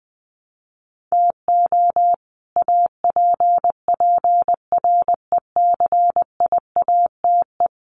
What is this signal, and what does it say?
Morse code, heard as a keyed tone, about 20 wpm, 700 Hz. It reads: TO APPRECIATE
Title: Sentences from Top 1300 Words 20wpm